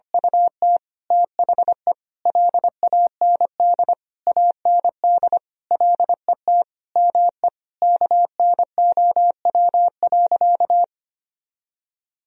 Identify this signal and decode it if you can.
Morse code, heard as a keyed tone, about 25 wpm, 700 Hz. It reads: UT THE LAND AND LET ME KNOW.
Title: Sentences from Top 1000 Words - Review Part 2 - 25wpm